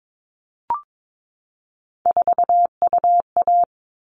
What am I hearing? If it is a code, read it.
Morse code, heard as a keyed tone, about 22 wpm, 700 Hz. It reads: 4UA
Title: All Cut Numbers - 3-Digit 22wpm